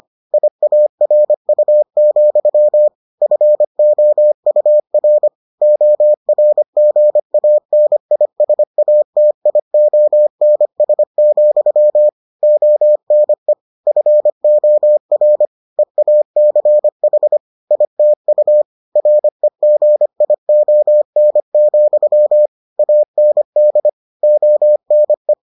Morse code, heard as a keyed tone, about 25 wpm, 600 Hz. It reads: IARU, FOUR ORGANISATIONS, ONE FOR EACH ITU REGION, AND ONE